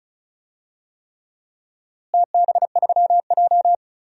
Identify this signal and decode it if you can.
Morse code, heard as a keyed tone, about 36 wpm, 700 Hz. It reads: TB3J
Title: DXCC-100 Call Signs - 4 Characters - Repeated 3x - 35wpm